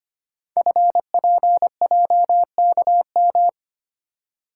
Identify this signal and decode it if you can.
Morse code, heard as a keyed tone, about 25 wpm, 700 Hz. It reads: FPJKM
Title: Code Groups - 5 Letter:Numbers - 25wpm